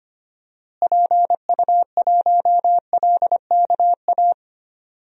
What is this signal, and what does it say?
Morse code, heard as a keyed tone, about 25 wpm, 700 Hz. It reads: PU1LKA
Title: DXCC-100 Call Signs - 6 Characters - Repeated 3x - 25wpm